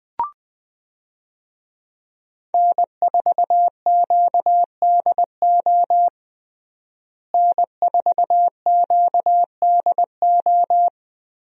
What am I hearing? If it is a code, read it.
Morse code, heard as a keyed tone, about 20 wpm, 700 Hz. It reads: N4QDO N4QDO
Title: USA Call Signs - 5 Characters - Repeated 2x - 20wpm